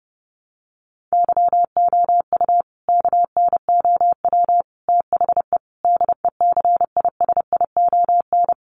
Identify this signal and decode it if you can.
Morse code, heard as a keyed tone, about 30 wpm, 700 Hz. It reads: YOU KNOW THE DECISION
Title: Sentences from Top 1000 Words 30wpm